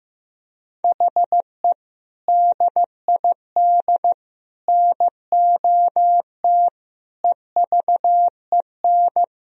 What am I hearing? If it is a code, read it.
Morse code, heard as a keyed tone, about 16 wpm, 700 Hz. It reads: HE DID NOT EVEN
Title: Sentences from Top 1000 Words - Review Part 1 - 15wpm